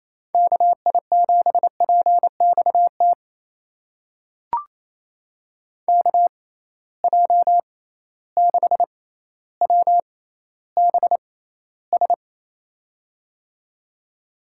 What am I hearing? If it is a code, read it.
Morse code, heard as a keyed tone, about 28 wpm, spaced as for 14 wpm, 700 Hz. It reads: KI7PXT KJ6WBS
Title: USA Call Signs - 6 Characters - 6x Character Spacing - 28wpm